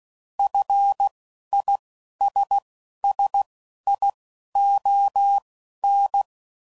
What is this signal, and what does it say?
Morse code, heard as a keyed tone, about 16 wpm, spaced as for 11 wpm, 800 Hz. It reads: FISSION